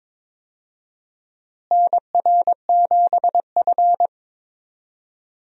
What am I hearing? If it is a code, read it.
Morse code, heard as a keyed tone, about 22 wpm, 700 Hz. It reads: NR7F
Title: USA Call Signs - 4 Characters - Repeated 2x - 22wpm